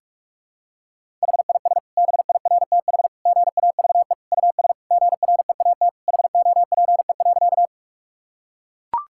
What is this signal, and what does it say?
Morse code, heard as a keyed tone, about 41 wpm, 700 Hz. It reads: HIS BIRTH GAVE US GREAT HOPE.
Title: Sentences from Top 1000 Words - Review Part 5 - 45wpm